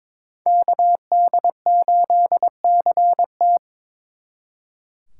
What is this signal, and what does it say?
Morse code, heard as a keyed tone, about 26 wpm, 700 Hz. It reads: KD8CT
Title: Speed Racing - USA 2x2 Call Signs 22wpm